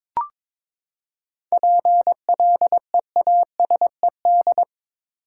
Morse code, heard as a keyed tone, about 22 wpm, 700 Hz. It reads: PLEASED
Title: Seven Letter Words - 22wpm